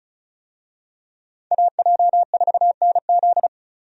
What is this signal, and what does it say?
Morse code, heard as a keyed tone, about 36 wpm, 700 Hz. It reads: AJ4NZ
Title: USA Call Signs - 5 Characters - Repeated 2x - 35wpm